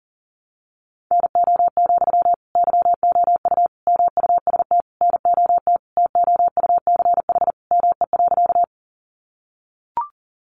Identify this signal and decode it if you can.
Morse code, heard as a keyed tone, about 40 wpm, 700 Hz. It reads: NO, YOU MUST NOT TOUCH ME.
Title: Sentences from Top 1200 Words 40wpm